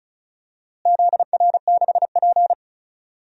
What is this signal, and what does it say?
Morse code, heard as a keyed tone, about 36 wpm, 700 Hz. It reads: ZR6P
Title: DXCC-100 Call Signs - 4 Characters - Repeated 3x - 35wpm